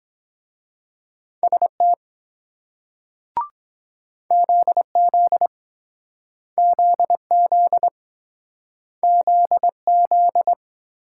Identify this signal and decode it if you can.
Morse code, heard as a keyed tone, about 23 wpm, 700 Hz. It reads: ST ZZ ZZ ZZ